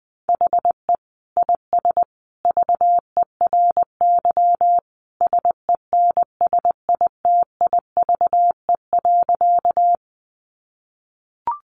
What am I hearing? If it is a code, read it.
Morse code, heard as a keyed tone, about 20 wpm, 700 Hz. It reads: HE IS VERY SENSITIVE.
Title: Sentences from Top 1400 Words 20wpm